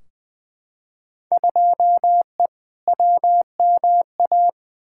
Morse code, heard as a keyed tone, about 20 wpm, 700 Hz. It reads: 2E WMA
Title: Field Day Exchanges Encore - 20wpm